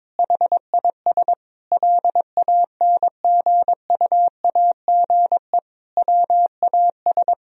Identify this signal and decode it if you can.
Morse code, heard as a keyed tone, about 22 wpm, 700 Hz. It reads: HIS LANGUAGE WAS
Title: Sentences from Top 1000 Words - Review Part 3 - 22wpm